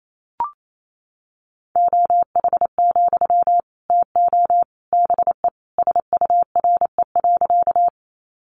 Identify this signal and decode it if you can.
Morse code, heard as a keyed tone, about 28 wpm, 700 Hz. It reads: OH, TO BE SURE.
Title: Sentences from Top 500 Words 28wpm